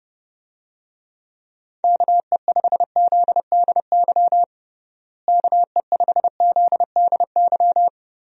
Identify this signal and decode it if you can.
Morse code, heard as a keyed tone, about 32 wpm, 700 Hz. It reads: KE5ZDY KE5ZDY
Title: USA Call Signs - 6 Characters - Repeated 3x - 30wpm